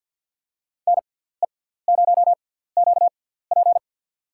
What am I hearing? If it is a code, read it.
Morse code, heard as a keyed tone, about 47 wpm, spaced as for 17 wpm, 700 Hz. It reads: NE0QP
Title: USA Call Signs - 5 Characters - 6x Character Spacing - 50wpm